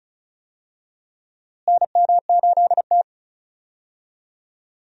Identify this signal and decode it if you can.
Morse code, heard as a keyed tone, about 34 wpm, 700 Hz. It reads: NM8T
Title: USA Call Signs - 4 Characters - Repeated 3x - 35wpm